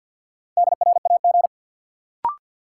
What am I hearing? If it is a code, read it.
Morse code, heard as a keyed tone, about 51 wpm, 700 Hz. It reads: DRAG
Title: Sentences from Top 1500 Words - New Word Drill Alt - 50wpm